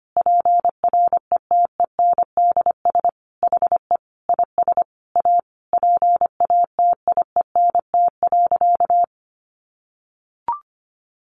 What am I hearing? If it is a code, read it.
Morse code, heard as a keyed tone, about 25 wpm, 700 Hz. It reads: PRETENDS HE IS A PATIENT.
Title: Additional Easy Phrases 25wpm